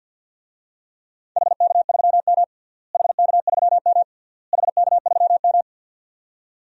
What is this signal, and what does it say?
Morse code, heard as a keyed tone, about 50 wpm, 700 Hz. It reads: HK3M HK3M HK3M
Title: DXCC-100 Call Signs - 4 Characters - Repeated 3x - 50wpm